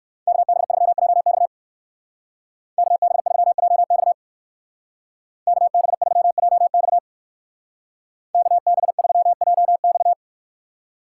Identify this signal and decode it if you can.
Morse code, heard as a keyed tone, about 56 wpm, 700 Hz. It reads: KB3JX KB3JX KB3JX KB3JX
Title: Speed Racing - USA 2x2 Call Signs 45wpm